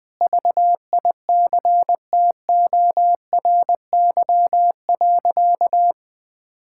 Morse code, heard as a keyed tone, about 21 wpm, 700 Hz. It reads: VICTORY.
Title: Sentences with 9-Word Phrases - Spoken Beginning - 20wpm